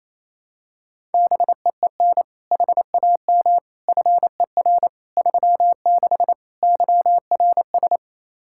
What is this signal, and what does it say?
Morse code, heard as a keyed tone, about 28 wpm, 700 Hz. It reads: BEEN HAM FER 36 YRS
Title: Practice QSO - Exchange 3 - More Info - 28wpm